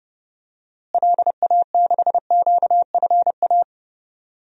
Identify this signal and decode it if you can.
Morse code, heard as a keyed tone, about 30 wpm, 700 Hz. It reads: LA6QFA
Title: DXCC-100 Call Signs - 6 Characters - Repeated 2x - 30wpm